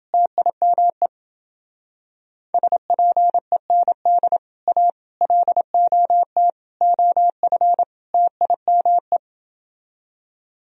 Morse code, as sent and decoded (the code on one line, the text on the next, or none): - .. -- . / ... .--. -. -.. / .- / .-.. --- - / --- ..-. / - .. -- .
TIME SPEND A LOT OF TIME